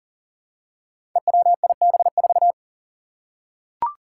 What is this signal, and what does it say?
Morse code, heard as a keyed tone, about 41 wpm, 700 Hz. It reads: EWIB4
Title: Five Challenging Letter-Number Combinations 40wpm